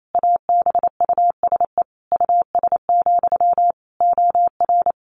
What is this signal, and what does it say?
Morse code, heard as a keyed tone, about 28 wpm, 700 Hz. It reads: ABUSE US, OR
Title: Sentences from Top 1400 Words 28wpm